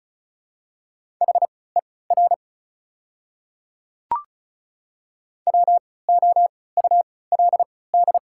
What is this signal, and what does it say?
Morse code, heard as a keyed tone, about 36 wpm, spaced as for 20 wpm, 690 Hz. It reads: HER WOULD
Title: 3x Character Spacing - Top 300 Words - 35wpm